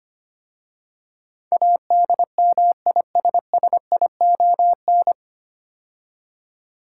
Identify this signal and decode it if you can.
Morse code, heard as a keyed tone, about 25 wpm, 700 Hz. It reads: ADMISSION
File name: practice / 25wpm / 9 Letter Words - First 3 Letters Given - 25wpm